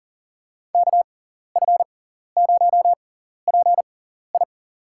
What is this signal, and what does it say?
Morse code, heard as a keyed tone, about 37 wpm, spaced as for 14 wpm, 700 Hz. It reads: KF0PI
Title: USA Call Signs - 5 Characters - 6x Character Spacing - 40wpm